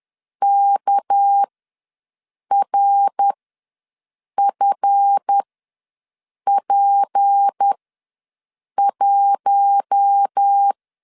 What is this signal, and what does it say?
Morse code, heard as a keyed tone, about 10 wpm, spaced as for 6 wpm, 800 Hz. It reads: KRFP1
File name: LEZIONI / 7 GF9Z / 7 GF ws 10wpm 3918